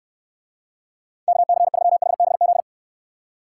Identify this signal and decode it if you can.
Morse code, heard as a keyed tone, about 68 wpm, 700 Hz. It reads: KC3HBZ